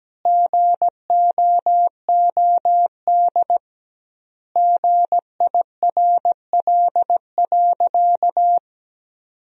Morse code, heard as a keyed tone, about 17 wpm, 700 Hz. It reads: GOOD GIRL.
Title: 2x Word-Spacing - Sentences from Top 400 Words - 17wpm